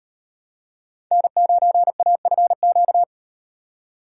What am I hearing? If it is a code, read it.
Morse code, heard as a keyed tone, about 36 wpm, 700 Hz. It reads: N9AFQ